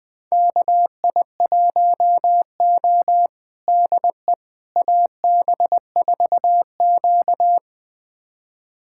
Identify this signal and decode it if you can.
Morse code, heard as a keyed tone, about 20 wpm, 700 Hz. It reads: KI1O DE AB4Q
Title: USA Callsign DE Callsign - 4 Characters - 20wpm